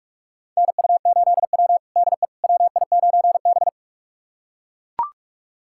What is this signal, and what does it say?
Morse code, heard as a keyed tone, about 43 wpm, 700 Hz. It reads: NU8W DE WI9B
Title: USA Callsign DE Callsign - 4 Characters - 45wpm